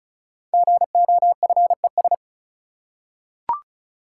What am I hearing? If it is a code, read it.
Morse code, heard as a keyed tone, about 36 wpm, 700 Hz. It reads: GOFES
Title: Code Groups - 5 Letters - 35wpm